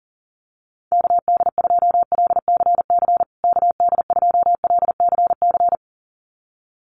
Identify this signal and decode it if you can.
Morse code, heard as a keyed tone, about 40 wpm, 700 Hz. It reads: KD2LCC KD2LCC
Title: USA Call Signs Repeated 2x 40wpm